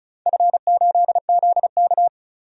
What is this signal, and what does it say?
Morse code, heard as a keyed tone, about 36 wpm, 690 Hz. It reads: F8ZK
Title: DXCC-100 Call Signs - 4 Characters - 35wpm